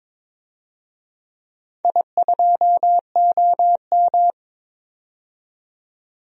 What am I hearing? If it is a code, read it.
Morse code, heard as a keyed tone, about 22 wpm, 700 Hz. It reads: I2OM
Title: DXCC-100 Call Signs - 4 Characters - 22wpm